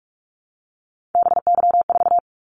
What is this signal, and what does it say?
Morse code, heard as a keyed tone, about 44 wpm, 700 Hz. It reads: BY4
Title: Callsign Prefixes for Top 100 DXCC - 45wpm